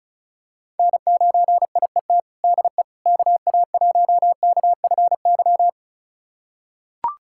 Morse code, heard as a keyed tone, about 34 wpm, 700 Hz. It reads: N9IET DE KA1KFY
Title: USA Callsign DE Callsign 35wpm